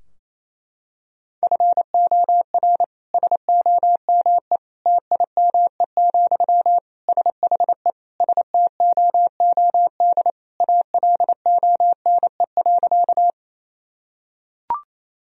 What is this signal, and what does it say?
Morse code, heard as a keyed tone, about 28 wpm, 700 Hz. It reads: FOR SOME TIME, SHE STOOD ALONE.
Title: Sentences from Top 1000 Words - Review Part 2 - 28wpm